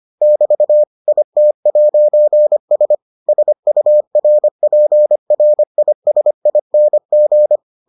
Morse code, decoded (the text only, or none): <BT> IT'S SURPRISING